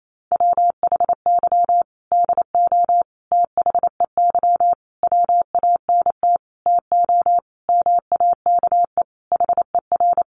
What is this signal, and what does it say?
Morse code, heard as a keyed tone, about 28 wpm, 700 Hz. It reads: WHY DO THEY WANT TO MAKE HER
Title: Sentences from Top 400 Words 28wpm